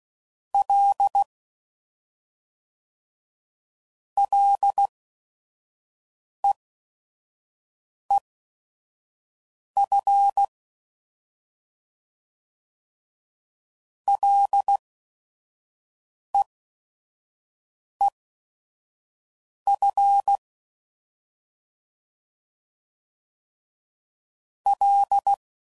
Morse code, heard as a keyed tone, about 16 wpm, spaced as for 5 wpm, 800 Hz. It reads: L LEEF LEEF L
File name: Prefixes / FLs & THs / Single Fs Ls